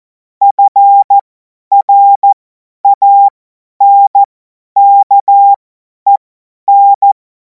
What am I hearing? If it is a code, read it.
Morse code, heard as a keyed tone, about 14 wpm, spaced as for 10 wpm, 800 Hz. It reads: FRANKEN
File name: Morsekurs / Witze / Witz 19. Mai 2026 50 F70